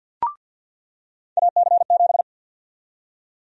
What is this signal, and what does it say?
Morse code, heard as a keyed tone, about 48 wpm, 700 Hz. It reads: AC7
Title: USA Call Sign Prefixes - 50wpm